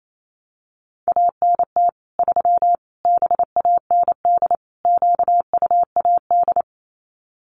Morse code, heard as a keyed tone, about 28 wpm, 700 Hz. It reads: ANT 3 BAND QUAD
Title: Common QSO Phrases 28wpm